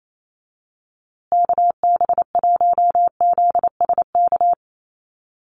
Morse code, heard as a keyed tone, about 28 wpm, 700 Hz. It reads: KB1ZSK
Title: USA Call Signs Repeated 2x 28wpm